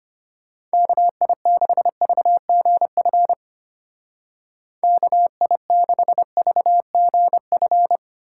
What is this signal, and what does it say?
Morse code, heard as a keyed tone, about 33 wpm, 700 Hz. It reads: KI6VGF KI6VGF